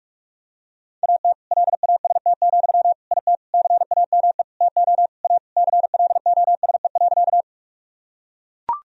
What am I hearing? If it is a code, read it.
Morse code, heard as a keyed tone, about 41 wpm, 700 Hz. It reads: AT LAST, IT CAME TO A CLOSE.
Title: Sentences from Top 1000 Words - Review Part 1 - 45wpm